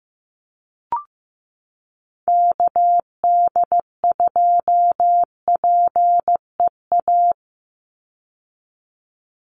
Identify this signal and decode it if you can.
Morse code, heard as a keyed tone, about 15 wpm, 700 Hz. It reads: KD2PEA
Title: USA Call Signs 15wpm